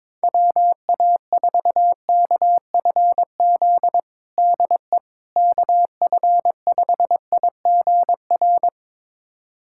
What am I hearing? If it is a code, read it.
Morse code, heard as a keyed tone, about 22 wpm, 700 Hz. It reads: WA4KFZ DE KF5IGR
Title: USA Callsign DE Callsign - 6 Characters - 22wpm